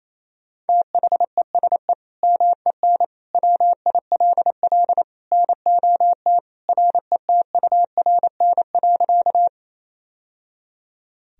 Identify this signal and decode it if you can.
Morse code, heard as a keyed tone, about 28 wpm, 700 Hz. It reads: THESE MEN WILL NOT RETURN.
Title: Sentences from Top 1000 Words - Review Part 1 - 28wpm